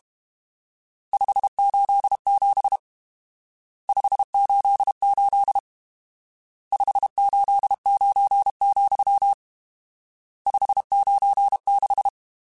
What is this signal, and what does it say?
Morse code, heard as a keyed tone, about 32 wpm, 800 Hz. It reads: 587 588 589, 596
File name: RSTs / RST-mp3s / Various RST @30